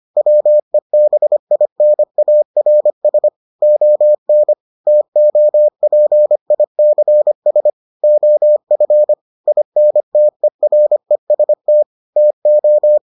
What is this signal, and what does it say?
Morse code, heard as a keyed tone, about 25 wpm, 600 Hz. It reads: WEBINARS ON TOPICS OF INTEREST TO